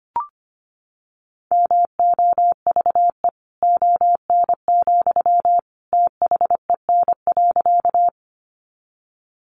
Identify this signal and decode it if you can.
Morse code, heard as a keyed tone, about 25 wpm, 700 Hz. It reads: MOVE ON, THEN.
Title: Sentences from Top 200 Words 25wpm